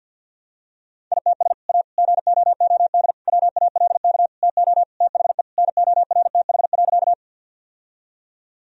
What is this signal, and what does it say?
Morse code, heard as a keyed tone, about 51 wpm, 700 Hz. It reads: ITS A GOOD WALK TO THE NORTH.
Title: Sentences from Top 1000 Words - Review Part 1 - 50wpm